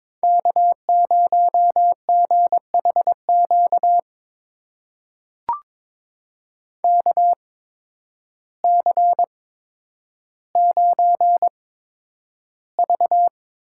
Morse code, heard as a keyed tone, about 23 wpm, spaced as for 10 wpm, 700 Hz. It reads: K0GHQ KC9V
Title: USA Call Signs - 5 Characters - 8x Character Spacing - 22wpm